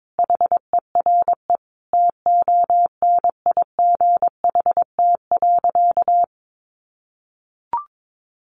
Morse code, heard as a keyed tone, about 22 wpm, 700 Hz. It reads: HERE TONIGHT.
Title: Sentences from Top 600 Words 22wpm